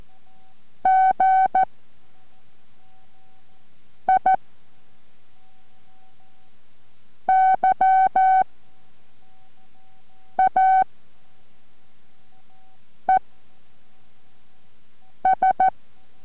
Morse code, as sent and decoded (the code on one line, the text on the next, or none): --. / .. / -.-- / .- / . / ...
G I Y A E S